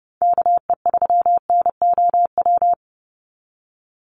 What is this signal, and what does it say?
Morse code, heard as a keyed tone, about 30 wpm, 700 Hz. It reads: KE3NOW
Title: USA Call Signs 30wpm